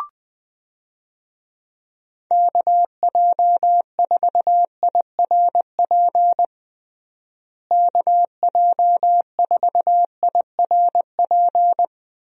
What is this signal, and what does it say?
Morse code, heard as a keyed tone, about 20 wpm, 700 Hz. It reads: KJ4IRP KJ4IRP
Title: USA Call Signs - 6 Characters - Repeated 3x - 20wpm